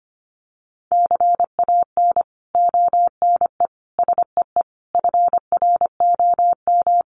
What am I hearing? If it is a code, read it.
Morse code, heard as a keyed tone, about 25 wpm, 700 Hz. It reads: CAN ONE SEE FROM
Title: Sentences from Top 100 Words 25wpm